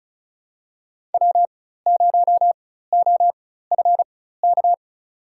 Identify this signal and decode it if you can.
Morse code, heard as a keyed tone, about 34 wpm, spaced as for 16 wpm, 700 Hz. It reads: W0OFK
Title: USA Call Signs - 5 Characters - 4x Character Spacing - 35wpm